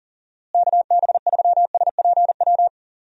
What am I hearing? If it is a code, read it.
Morse code, heard as a keyed tone, about 40 wpm, 700 Hz. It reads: KB3SPW
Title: USA Call Signs - 6 Characters - Repeated 3x - 40wpm